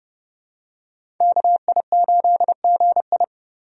Morse code, heard as a keyed tone, about 29 wpm, 700 Hz. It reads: KI8GI
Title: USA Call Signs - 5 Characters - Repeated 2x - 30wpm